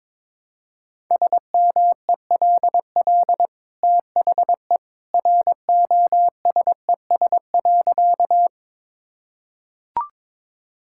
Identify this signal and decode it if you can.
Morse code, heard as a keyed tone, about 22 wpm, 690 Hz. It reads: SMELL THE ROSES.
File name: practice / 22wpm / Easy Sentences with Hints - 22wpm